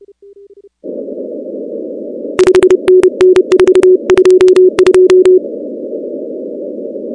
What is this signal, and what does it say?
Morse code, heard as a keyed tone, about 31 wpm, 370 Hz. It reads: 5NN 422
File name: contests / 2019-cw-wpx-kt5j-k5pi- -2330 0-16